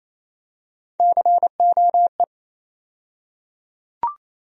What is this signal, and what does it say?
Morse code, heard as a keyed tone, about 28 wpm, 700 Hz. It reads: COE
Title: USA Call Sign Postfixes - 3 Letters 28wpm